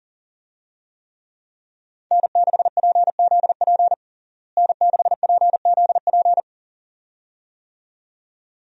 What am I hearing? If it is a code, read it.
Morse code, heard as a keyed tone, about 40 wpm, 700 Hz. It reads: N6PZP N6PZP
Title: USA Call Signs - 5 Characters - Repeated 2x - 40wpm